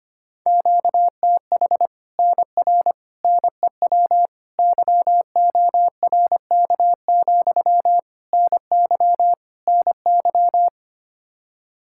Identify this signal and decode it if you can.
Morse code, heard as a keyed tone, about 25 wpm, 700 Hz. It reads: QTH NR NEW YORK, NY NY
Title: Practice QSO - Exchange 1 - Beginning - 25wpm